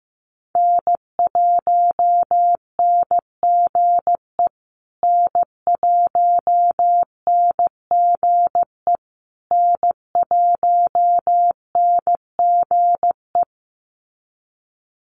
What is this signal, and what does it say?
Morse code, heard as a keyed tone, about 15 wpm, 700 Hz. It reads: N1NGE N1NGE N1NGE
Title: USA Call Signs Repeated 3x 15wpm